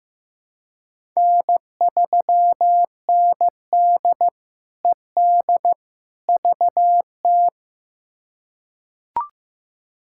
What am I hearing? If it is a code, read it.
Morse code, heard as a keyed tone, about 15 wpm, 700 Hz. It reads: N3ND ED VT
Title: K1USN SST QSO Practice - 15wpm